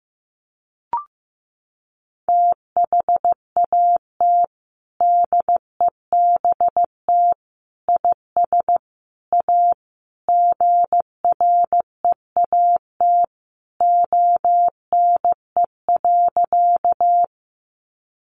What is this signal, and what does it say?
Morse code, heard as a keyed tone, about 15 wpm, 700 Hz. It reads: THAT DEBT IS A GREAT ONE.
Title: Sentences from Top 1300 Words 15wpm